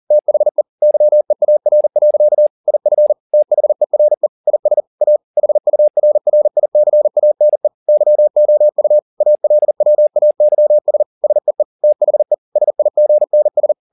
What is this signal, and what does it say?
Morse code, heard as a keyed tone, about 40 wpm, 600 Hz. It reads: THE YEAR. IF THERE IS A HURRICANE YOU ALWAYS SEE THE SIGNS